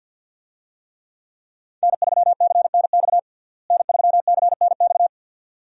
Morse code, heard as a keyed tone, about 53 wpm, 700 Hz. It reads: N3CNX N3CNX
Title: USA Call Signs - 5 Characters - Repeated 2x - 50wpm